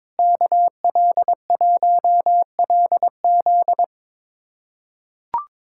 Morse code, heard as a keyed tone, about 22 wpm, 700 Hz. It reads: KL1LZ
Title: USA Callsign DE Callsign - 5 Characters - 22wpm